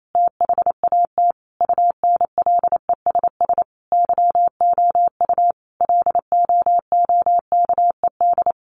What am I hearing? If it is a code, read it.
Morse code, heard as a keyed tone, about 28 wpm, 700 Hz. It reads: THAT UNLESS YOU LOOKED
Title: A Sound Of Thunder 28wpm